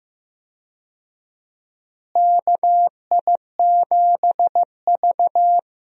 Morse code, heard as a keyed tone, about 15 wpm, 700 Hz. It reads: KI7V
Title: USA Call Signs - 4 Characters - Repeated 2x - 15wpm